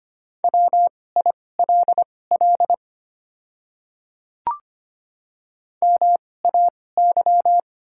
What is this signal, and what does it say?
Morse code, heard as a keyed tone, about 25 wpm, spaced as for 18 wpm, 690 Hz. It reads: WILL MAY
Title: 2x Character Spacing - Top 200 Words - 25wpm